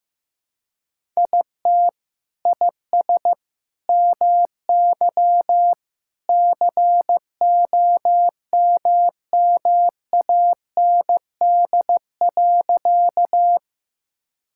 Morse code, heard as a keyed tone, about 15 wpm, 700 Hz. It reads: IT IS MY COMMAND.
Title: Sentences from Top 1600 Words 15wpm